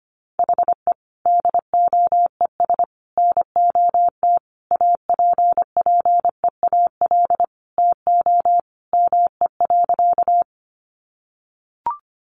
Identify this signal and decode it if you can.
Morse code, heard as a keyed tone, about 25 wpm, 700 Hz. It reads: HE DOES NOT APPEAL TO ME.
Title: Sentences from Top 1300 Words 25wpm